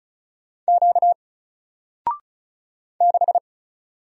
Morse code, heard as a keyed tone, about 36 wpm, 700 Hz. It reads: Q 6
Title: Single Letter-Number 35wpm